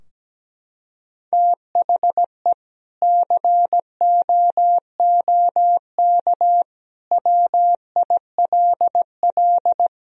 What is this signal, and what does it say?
Morse code, heard as a keyed tone, about 17 wpm, 700 Hz. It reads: THE COOK WILL
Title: Sentences from Top 1000 Words - Review Part 4 - 17wpm